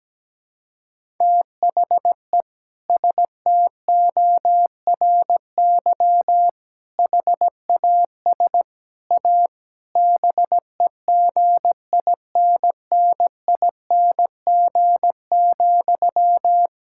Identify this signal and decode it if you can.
Morse code, heard as a keyed tone, about 17 wpm, 700 Hz. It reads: THE STORY HAS A BEGINNING,
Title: Sentences from Top 1000 Words - Review Part 6 - 17wpm